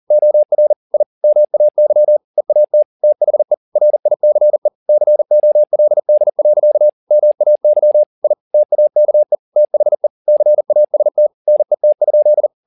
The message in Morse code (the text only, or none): OR I MAY EAT THE RICE COLD. MAY I TAKE THE CAST NET?